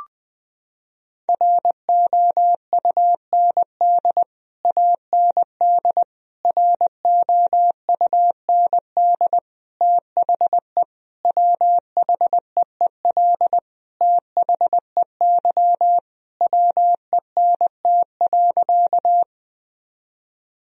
Morse code, heard as a keyed tone, about 21 wpm, 700 Hz. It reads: ROUND AND ROUND THE WHEEL THEY WENT.
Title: Sentences from Top 1000 Words - Review Part 3 - 20wpm